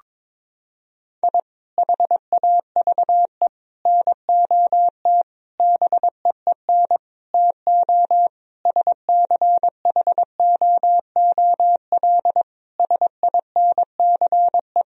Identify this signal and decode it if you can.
Morse code, heard as a keyed tone, about 22 wpm, 700 Hz. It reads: I HAVE NOT BEEN TO SCHOOL SINCE
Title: Sentences from Top 1000 Words - Review Part 1 - 22wpm